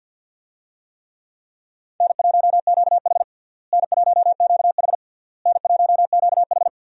Morse code, heard as a keyed tone, about 52 wpm, 700 Hz. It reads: N1QH N1QH N1QH
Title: USA Call Signs - 4 Characters - Repeated 3x - 50wpm